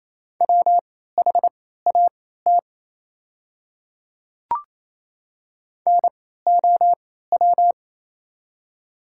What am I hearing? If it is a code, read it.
Morse code, heard as a keyed tone, about 27 wpm, spaced as for 16 wpm, 700 Hz. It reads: WHAT NOW
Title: 3x Character Spacing - Top 100 Words - 28wpm